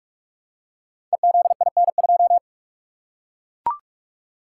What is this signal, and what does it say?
Morse code, heard as a keyed tone, about 44 wpm, 700 Hz. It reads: EZIN2
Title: Code Groups - 5 Letter:Numbers - 45wpm